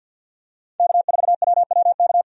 Morse code, heard as a keyed tone, about 46 wpm, 700 Hz. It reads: K4WWK
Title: USA Call Signs - 5 Characters - 50wpm